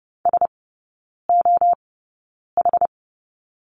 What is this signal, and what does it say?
Morse code, heard as a keyed tone, about 30 wpm, spaced as for 9 wpm, 700 Hz. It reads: SOH
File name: course / 30:10wpm / 087 Lesson 15e - Number 5 - Three Characters 30wpm